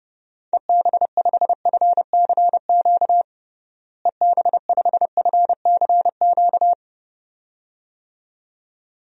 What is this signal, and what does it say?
Morse code, heard as a keyed tone, about 30 wpm, 700 Hz. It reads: EB5FCQ EB5FCQ
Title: DXCC-100 Call Signs - 6 Characters - Repeated 2x - 30wpm